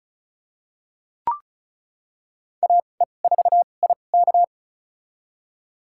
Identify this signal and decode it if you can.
Morse code, heard as a keyed tone, about 34 wpm, spaced as for 25 wpm, 700 Hz. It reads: AE4IK
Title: USA Call Signs - 5 Characters - 2x Character Spacing - 35wpm